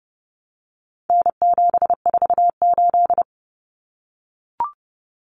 Morse code, heard as a keyed tone, about 29 wpm, 700 Hz. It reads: N748
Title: Common Cut Numbers - 4-Digit 30wpm